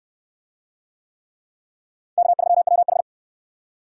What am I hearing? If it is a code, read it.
Morse code, heard as a keyed tone, about 65 wpm, 700 Hz. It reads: K3WH